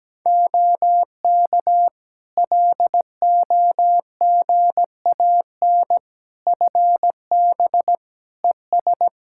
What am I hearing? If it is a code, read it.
Morse code, heard as a keyed tone, about 17 wpm, 700 Hz. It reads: OK LOGAN FB ES
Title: Practice QSO - Exchange 2 - Info - 17wpm